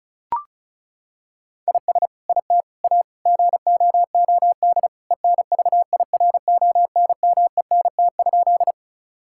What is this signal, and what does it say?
Morse code, heard as a keyed tone, about 36 wpm, 700 Hz. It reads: IS IT A GOOD ENVIRONMENT?
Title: Sentences from Top 1000 Words - Review Part 5 - 35wpm